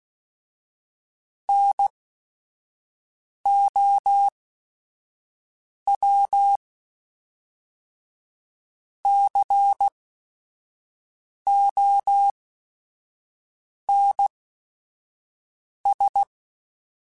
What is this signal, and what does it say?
Morse code, heard as a keyed tone, about 16 wpm, spaced as for 5 wpm, 800 Hz. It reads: NOW CONS
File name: Technical Subjects / Technical Subjects-mp3s / OR Gate @5